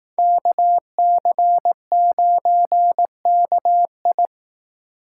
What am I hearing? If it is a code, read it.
Morse code, heard as a keyed tone, about 19 wpm, 700 Hz. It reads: KC9KI
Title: Speed Racing - USA 2x2 Call Signs 15wpm